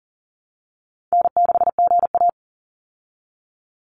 Morse code, heard as a keyed tone, about 40 wpm, 700 Hz. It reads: N6GA
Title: Easy Call Signs Repeated 3x - 40wpm